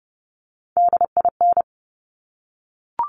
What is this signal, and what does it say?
Morse code, heard as a keyed tone, about 30 wpm, 700 Hz. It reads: DIN
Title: Top Three Letter Combinations in Words 30wpm